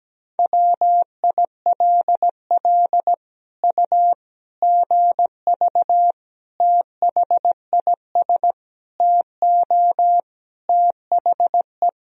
Morse code, heard as a keyed tone, about 17 wpm, 700 Hz. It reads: WILL U GV THIS TO THE
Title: Sentences from Top 200 Words and Abbreviations 17wpm